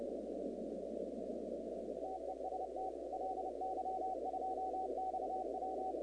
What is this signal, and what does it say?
Morse code, heard as a keyed tone, about 29 wpm, 750 Hz, with noise. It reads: TEST LY2XW
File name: contests / 2019-cw-wpx-kt5j-k5pi- -2330 0-38